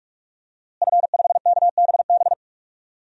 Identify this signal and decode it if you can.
Morse code, heard as a keyed tone, about 46 wpm, 700 Hz. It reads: F5KBB
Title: DXCC-100 Call Signs - 5 Characters - Repeated 3x - 45wpm